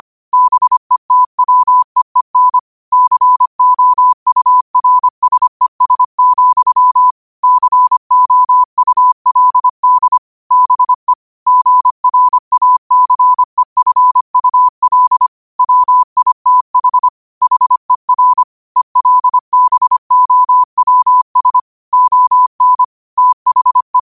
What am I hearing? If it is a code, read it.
Morse code, heard as a keyed tone, about 25 wpm, 1000 Hz. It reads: BETWEEN COURSES, COULD BE GRACEFUL WITH HER ELBOWS ON THE
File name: qotdfiles / 25wpm